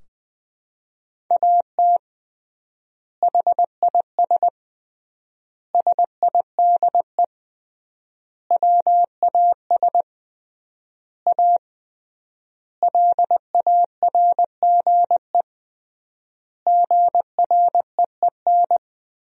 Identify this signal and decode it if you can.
Morse code, heard as a keyed tone, about 20 wpm, 700 Hz. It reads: AT HIS SIDE WAS A LARGE GREEN
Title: 3x Word-Spacing - Sentences from Top 500 Words - 20wpm